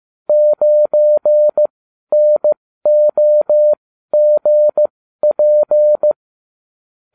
Morse code, heard as a keyed tone, about 15 wpm, spaced as for 12 wpm, 600 Hz. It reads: 9NOGP